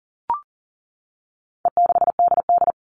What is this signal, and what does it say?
Morse code, heard as a keyed tone, about 40 wpm, 700 Hz. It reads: E6DD
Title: All Cut Numbers - 4-Digit 40wpm